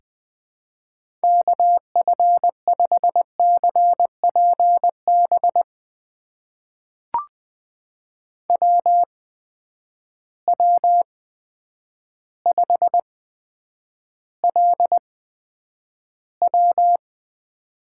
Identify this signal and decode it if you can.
Morse code, heard as a keyed tone, about 20 wpm, spaced as for 9 wpm, 700 Hz. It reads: KF5CPB WW5LW
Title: USA Call Signs - 8x Character Spacing - 20wpm